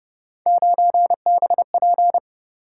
Morse code, heard as a keyed tone, about 31 wpm, 700 Hz. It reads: 9BP